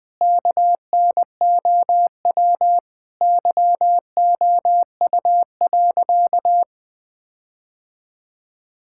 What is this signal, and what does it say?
Morse code, heard as a keyed tone, about 20 wpm, 700 Hz. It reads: KNOW YOU.
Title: Sentences from Top 1600 Words 20wpm